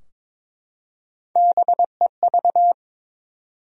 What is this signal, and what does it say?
Morse code, heard as a keyed tone, about 23 wpm, 700 Hz. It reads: BEV
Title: USA Common Names - Max 3 Letters - 22wpm